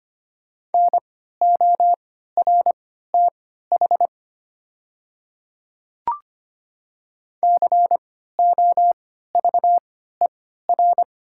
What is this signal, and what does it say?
Morse code, heard as a keyed tone, about 25 wpm, spaced as for 14 wpm, 700 Hz. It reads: NORTH COVER
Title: 3x Character Spacing - Top 300 Words - 25wpm